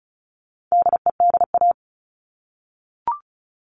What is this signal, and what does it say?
Morse code, heard as a keyed tone, about 34 wpm, 700 Hz. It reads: DEDA
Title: All Cut Numbers - 4-Digit 35wpm